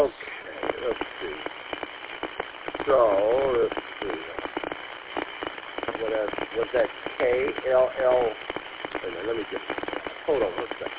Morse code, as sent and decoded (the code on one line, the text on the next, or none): none